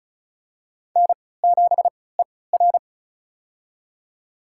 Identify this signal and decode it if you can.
Morse code, heard as a keyed tone, about 34 wpm, spaced as for 20 wpm, 700 Hz. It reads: N7ER
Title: USA Call Signs - 4 Characters - 3x Character Spacing - 35wpm